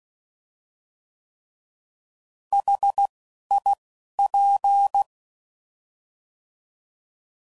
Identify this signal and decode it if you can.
Morse code, heard as a keyed tone, about 16 wpm, 800 Hz. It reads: H I P